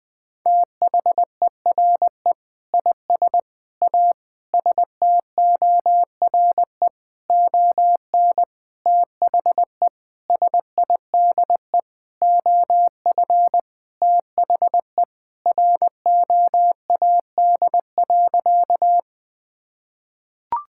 Morse code, heard as a keyed tone, about 20 wpm, 700 Hz. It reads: THERE IS A STORE ON THE SIDE OF THE ROAD.
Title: Sentences with 6-Word Phrases - 20wpm